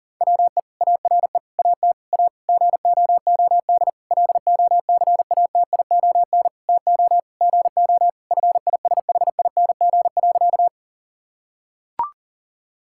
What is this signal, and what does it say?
Morse code, heard as a keyed tone, about 40 wpm, 700 Hz. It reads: WE ARE AT A GOOD LOCATION TO GO FISHING.
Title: Sentences from Top 1000 Words - Review Part 6 - 40wpm